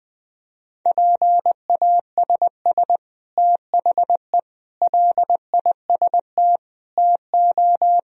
Morse code, heard as a keyed tone, about 19 wpm, 700 Hz. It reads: PASS THE LIST TO
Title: Sentences from Top 1000 Words - Review Part 2 - 20wpm